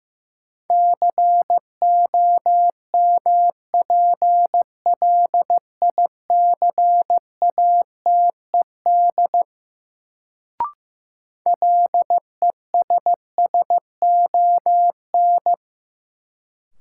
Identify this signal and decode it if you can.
Morse code, heard as a keyed tone, about 15 wpm, 700 Hz. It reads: COMPLICATED LESSON
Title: Sentences from Top 1200 Words - New Word Drill - 15wpm